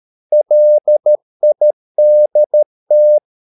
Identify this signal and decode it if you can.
Morse code, heard as a keyed tone, about 13 wpm, 600 Hz. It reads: LIDT
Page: Ferie 13 wpm